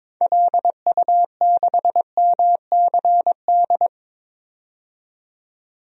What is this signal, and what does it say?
Morse code, heard as a keyed tone, about 22 wpm, 700 Hz. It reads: LU6MCD
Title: DXCC-100 Call Signs - 6 Characters - 22wpm